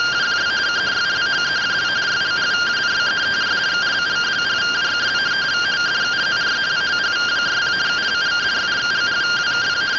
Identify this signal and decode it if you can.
Two-tone teleprinter signal, about 170 Hz shift.
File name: findings / 202 coronal iodine 3mm.gen-05 rtty rx